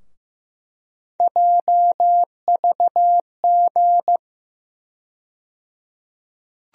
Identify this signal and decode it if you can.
Morse code, heard as a keyed tone, about 15 wpm, 700 Hz. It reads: JVG
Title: USA Call Sign Postfixes - 3 Letters 15wpm